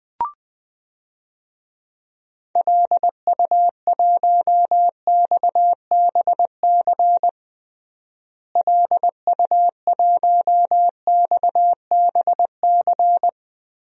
Morse code, heard as a keyed tone, about 21 wpm, 700 Hz. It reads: LU1XBC LU1XBC
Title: DXCC-100 Call Signs - 6 Characters - Repeated 3x - 20wpm